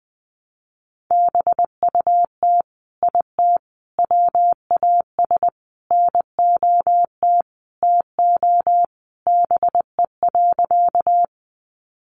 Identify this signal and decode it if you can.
Morse code, heard as a keyed tone, about 20 wpm, 700 Hz. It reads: BUT IT WAS NOT TO BE.
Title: Sentences from Top 600 Words 20wpm